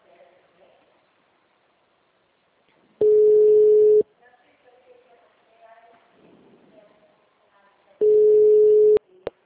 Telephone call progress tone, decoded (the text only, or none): ringback tone